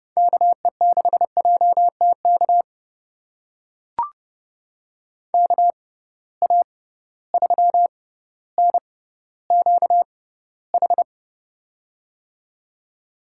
Morse code, heard as a keyed tone, about 30 wpm, spaced as for 15 wpm, 700 Hz. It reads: KE6JTK KA3NQH
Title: USA Call Signs - 6 Characters - 6x Character Spacing - 30wpm